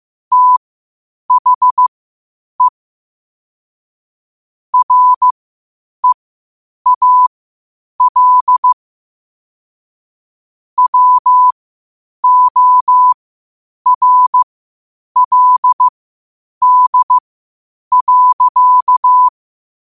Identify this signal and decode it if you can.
Morse code, heard as a keyed tone, about 15 wpm, spaced as for 8 wpm, 1000 Hz. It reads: THE REAL WORLD.
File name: qotdfiles / 8wpm